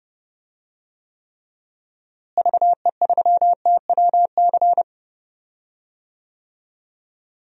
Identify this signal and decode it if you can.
Morse code, heard as a keyed tone, about 32 wpm, 700 Hz. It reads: VE3TWC
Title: DXCC-100 Call Signs - 6 Characters - Repeated 3x - 30wpm